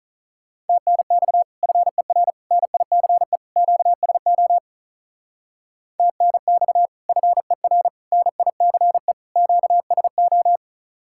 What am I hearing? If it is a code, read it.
Morse code, heard as a keyed tone, about 43 wpm, 700 Hz. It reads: TNX FER NICE QSO TNX FER NICE QSO
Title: Speed-Racing Common QSO Phrases Encore - 35wpm